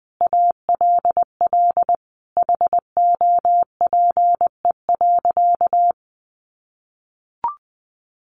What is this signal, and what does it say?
Morse code, heard as a keyed tone, about 20 wpm, 700 Hz. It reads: ALL HOPE.
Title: Sentences from Top 1400 Words 20wpm